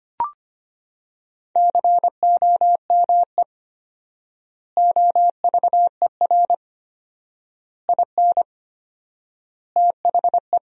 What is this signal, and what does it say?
Morse code, heard as a keyed tone, about 25 wpm, 700 Hz. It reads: COME OVER IN THE
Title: 4x Word-Spacing - Sentences from Top 300 Words - 25wpm